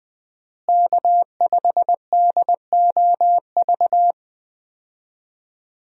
Morse code, heard as a keyed tone, about 21 wpm, 700 Hz. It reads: K5DOV
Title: USA Call Signs - 5 Characters - Repeated 3x - 20wpm